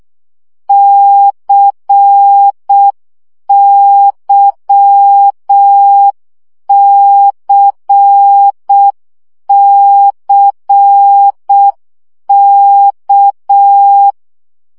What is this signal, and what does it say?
Morse code, heard as a keyed tone, about 6 wpm, 800 Hz. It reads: CYCCK
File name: CW gyakorlas k c y 5 baud